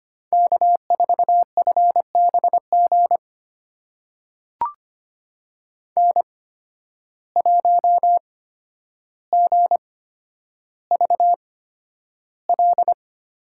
Morse code, heard as a keyed tone, about 25 wpm, spaced as for 11 wpm, 700 Hz. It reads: K4FBG N1GVL
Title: USA Call Signs - 5 Characters - 8x Character Spacing - 25wpm